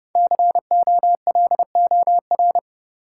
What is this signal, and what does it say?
Morse code, heard as a keyed tone, about 30 wpm, 700 Hz. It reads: COLOR
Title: Sentences from Top 1000 Words - Review Part 2 - New Word Drill - 30wpm